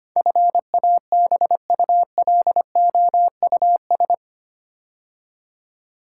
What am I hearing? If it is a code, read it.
Morse code, heard as a keyed tone, about 25 wpm, 700 Hz. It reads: FABULOUS
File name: 8 Letter Words - First 3 Letters Given - 25wpm